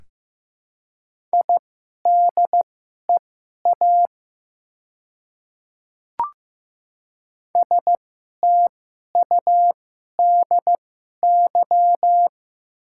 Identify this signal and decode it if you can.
Morse code, heard as a keyed tone, about 15 wpm, spaced as for 11 wpm, 700 Hz. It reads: IDEA STUDY
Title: 2x Character Spacing - Top 500 Words - 15wpm